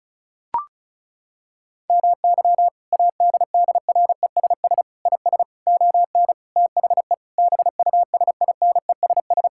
Morse code, heard as a keyed tone, about 36 wpm, 700 Hz. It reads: MY ADDRESS IS ON THE BUSINESS
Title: Sentences from Top 1000 Words - Review Part 6 - 35wpm